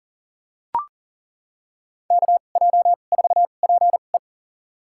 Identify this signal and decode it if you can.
Morse code, heard as a keyed tone, about 39 wpm, spaced as for 29 wpm, 700 Hz. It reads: KJ4PE
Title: USA Call Signs - 5 Characters - 2x Character Spacing - 40wpm